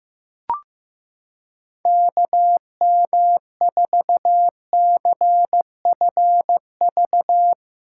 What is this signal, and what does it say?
Morse code, heard as a keyed tone, about 15 wpm, 700 Hz. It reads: KM4CFV
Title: USA Callsign DE Callsign - 6 Characters - 15wpm